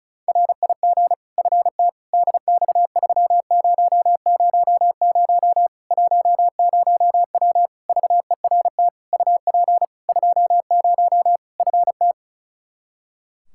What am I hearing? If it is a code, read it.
Morse code, heard as a keyed tone, about 36 wpm, 700 Hz. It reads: RIG FT DX3000 10W VERT UP 20 FT
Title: Practice QSO - Exchange 2 - Info - 35wpm